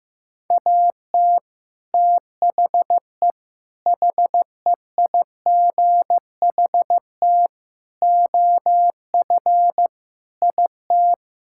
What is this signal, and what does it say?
Morse code, heard as a keyed tone, about 15 wpm, 700 Hz. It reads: AT THE HEIGHT OF IT
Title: Sentences from Top 1600 Words 15wpm